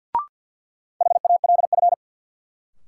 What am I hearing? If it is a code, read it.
Morse code, heard as a keyed tone, about 52 wpm, 700 Hz. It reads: HALF
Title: Sets of 3 Words - New Word Drill - 50wpm